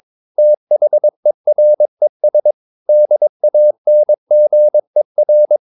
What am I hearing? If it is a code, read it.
Morse code, heard as a keyed tone, about 22 wpm, 600 Hz. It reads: THERES DANGER
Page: Romanen Never af Ken Follet på engelsk.